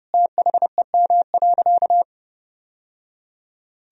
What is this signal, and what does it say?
Morse code, heard as a keyed tone, about 30 wpm, 700 Hz. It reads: THEM.
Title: Sentences from Top 1000 Words - Review Part 1 - 30wpm